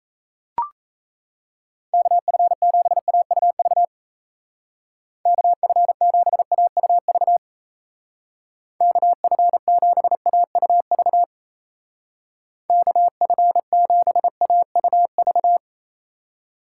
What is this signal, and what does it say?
Morse code, heard as a keyed tone, about 34 wpm, 700 Hz. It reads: KF7AUV KF7AUV KF7AUV KF7AUV
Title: Speed Racing - USA 2x3 Call Signs 28wpm